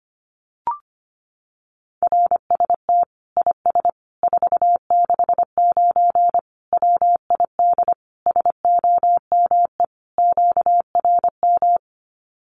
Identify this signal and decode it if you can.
Morse code, heard as a keyed tone, about 25 wpm, 700 Hz. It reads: RST IS 469 WID SOME QRM
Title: Common QSO Phrases 25wpm